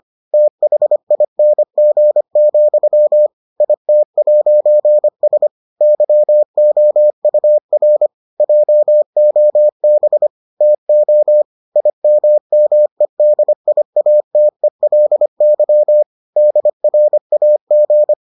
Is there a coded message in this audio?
Yes: Morse code from 0.3 to 18.1 s, heard as a keyed tone, about 25 wpm, 600 Hz. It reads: THING, IT'S YOUR JOB TO IMMEDIATELY DRAG